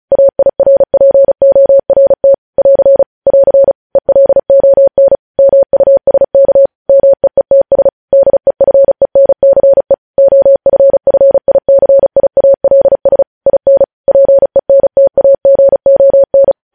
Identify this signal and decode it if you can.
Morse code, heard as a keyed tone, about 36 wpm, 550 Hz. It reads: AIRPORT <AR> <AR> ELON MUSK MEETS DEFENCE OFFICIALS IN PENTAGON
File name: podcast / News headlines-1x-35x35